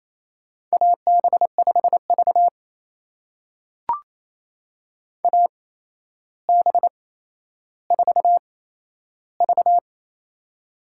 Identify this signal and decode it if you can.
Morse code, heard as a keyed tone, about 28 wpm, spaced as for 12 wpm, 700 Hz. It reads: AB5V AB4V
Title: USA Call Signs - 4 Characters - 8x Character Spacing - 28wpm